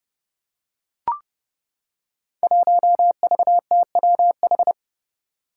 Morse code, heard as a keyed tone, about 28 wpm, 700 Hz. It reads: 1VTWH
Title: Five Challenging Letter-Number Combinations 30wpm